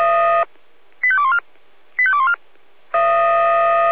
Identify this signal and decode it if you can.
Five-tone selective call (ZVEI), heard as quick runs of five tones, tones of about 70 ms. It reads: 85214 85214
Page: SDR/ZVEI